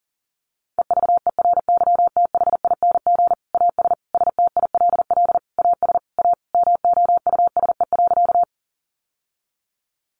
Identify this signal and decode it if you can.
Morse code, heard as a keyed tone, about 40 wpm, 700 Hz. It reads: EVERYTHING AS STILL AS A MOUSE.
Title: Sentences from Top 700 Words 40wpm